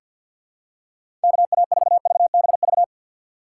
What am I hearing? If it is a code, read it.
Morse code, heard as a keyed tone, about 45 wpm, 700 Hz. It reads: KA4VBV
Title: USA Call Signs - 2x Character Spacing - 50wpm